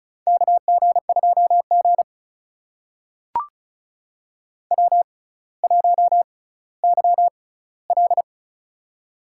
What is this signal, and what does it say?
Morse code, heard as a keyed tone, about 36 wpm, spaced as for 18 wpm, 700 Hz. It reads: KG2G W1YL
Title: USA Call Signs - 4 Characters - 6x Character Spacing - 35wpm